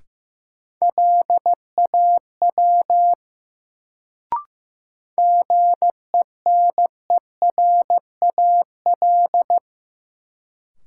Morse code, heard as a keyed tone, about 16 wpm, 700 Hz. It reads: LAW GENERAL
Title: Sentences from Top 600 Words - New Word Drill - 15wpm